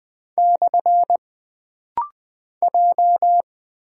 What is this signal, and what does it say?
Morse code, heard as a keyed tone, about 20 wpm, 700 Hz. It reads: / J
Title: Single Character (Letter:Number:Punctuation:Prosign) - Rapid-Fire - 20wpm